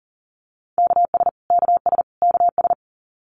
Morse code, heard as a keyed tone, about 40 wpm, 700 Hz. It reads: KS KS KS
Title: US States Repeated 3x 40wpm